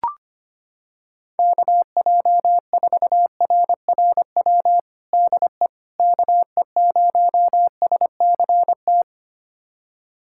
Morse code, heard as a keyed tone, about 25 wpm, 700 Hz. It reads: KJ4RRW DE KE0SCT
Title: USA Callsign DE Callsign - 6 Characters - 25wpm